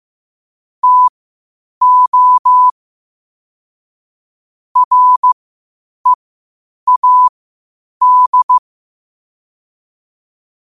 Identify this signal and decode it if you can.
Morse code, heard as a keyed tone, about 15 wpm, spaced as for 8 wpm, 1000 Hz. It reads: TO READ
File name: qotdfiles / 8wpm